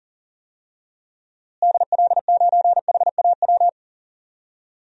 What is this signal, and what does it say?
Morse code, heard as a keyed tone, about 40 wpm, 700 Hz. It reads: DL9HAW
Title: DXCC-100 Call Signs - 6 Characters - Repeated 3x - 40wpm